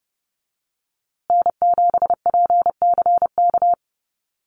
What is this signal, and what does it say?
Morse code, heard as a keyed tone, about 30 wpm, 700 Hz. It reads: N7PCK
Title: USA Call Signs Repeated 2x 30wpm